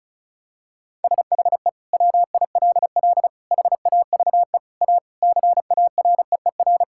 Morse code, heard as a keyed tone, about 36 wpm, 700 Hz. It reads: SHE WILL HAVE A CAREER
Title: Sentences from Top 1000 Words - Review Part 6 - 35wpm